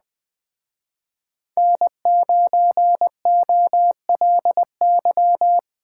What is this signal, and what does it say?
Morse code, heard as a keyed tone, about 21 wpm, 700 Hz. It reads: N9OLY
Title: Speed Racing - USA Call Signs 17wpm